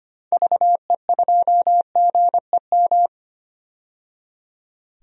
Morse code, heard as a keyed tone, about 25 wpm, 700 Hz. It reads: VE2GEM
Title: DXCC-100 Call Signs - 6 Characters - 25wpm